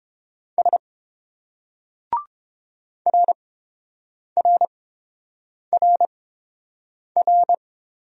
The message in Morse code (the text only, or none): S RRRR